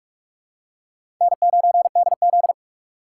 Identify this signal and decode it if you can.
Morse code, heard as a keyed tone, about 44 wpm, 700 Hz. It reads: N9DZ
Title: USA Call Signs - 4 Characters - 6x Character Spacing - 45wpm